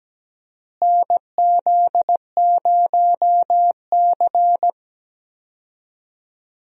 Morse code, heard as a keyed tone, about 17 wpm, 700 Hz. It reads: NZ0C
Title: USA Call Signs - 4 Characters - Repeated 2x - 17wpm